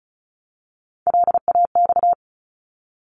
Morse code, heard as a keyed tone, about 36 wpm, 700 Hz. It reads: LAX
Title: ARRL Sections 35wpm